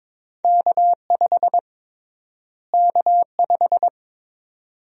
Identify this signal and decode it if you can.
Morse code, heard as a keyed tone, about 22 wpm, 700 Hz. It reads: K5 K5
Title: USA Call Sign Prefixes - Repeated 2x - 22wpm